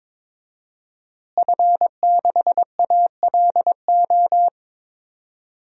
Morse code, heard as a keyed tone, about 22 wpm, 700 Hz. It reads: F6ALO
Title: DXCC-100 Call Signs - 5 Characters - Repeated 3x - 22wpm